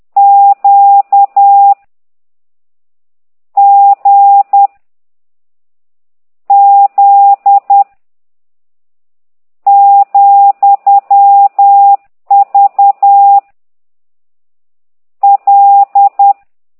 Morse code, heard as a keyed tone, about 10 wpm, 800 Hz. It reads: Q G Z , V L